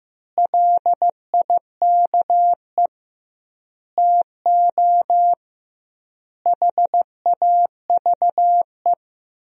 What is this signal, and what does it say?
Morse code, heard as a keyed tone, about 15 wpm, 700 Hz. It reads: LIKE TO HAVE
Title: 2x Word-Spacing - Sentences from Top 100 Words - 15wpm